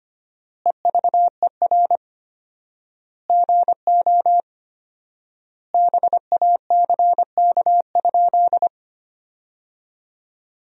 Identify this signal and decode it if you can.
Morse code, heard as a keyed tone, about 25 wpm, 700 Hz. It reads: EVER GO BACK?
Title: 4x Word-Spacing - Sentences from Top 400 Words - 25wpm